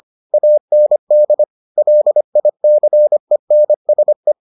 Morse code, heard as a keyed tone, about 25 wpm, 600 Hz. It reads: AND LICENSE